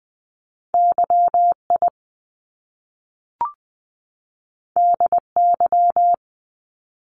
Morse code, heard as a keyed tone, about 20 wpm, 700 Hz. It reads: YI DY
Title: Top Two Letter Combinations in Words 20wpm